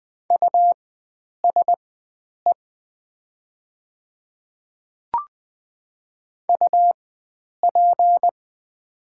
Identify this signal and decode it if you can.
Morse code, heard as a keyed tone, about 20 wpm, spaced as for 9 wpm, 700 Hz. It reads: USE UP
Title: 4x Character Spacing - Top 100 Words - 20wpm